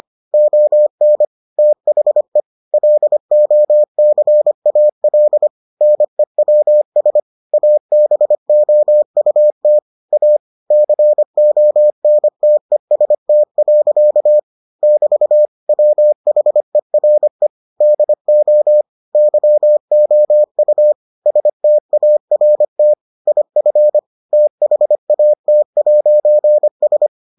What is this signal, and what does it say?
Morse code, heard as a keyed tone, about 25 wpm, 600 Hz. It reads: ON THE LOCAL NEWS ABOUT A CONTEST. <BT> WHERE DO YOU START IF THAT'S